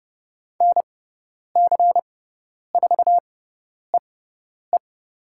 Morse code, heard as a keyed tone, about 30 wpm, spaced as for 10 wpm, 700 Hz. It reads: NC4EE
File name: Lesson 18g - Letter m - Call Signs 30wpm